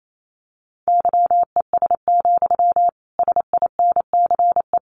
Morse code, heard as a keyed tone, about 28 wpm, 700 Hz. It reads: YES, SINCE
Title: Sentences from Top 1300 Words 28wpm